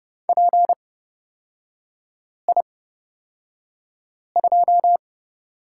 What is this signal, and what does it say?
Morse code, heard as a keyed tone, about 30 wpm, spaced as for 5 wpm, 700 Hz. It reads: PI2
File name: Lesson 38e - Letter q - Three Characters 30wpm